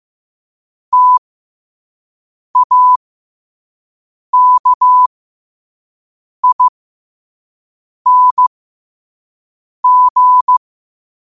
Morse code, heard as a keyed tone, about 15 wpm, spaced as for 5 wpm, 1000 Hz. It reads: TAKING
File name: qotdfiles / 5wpm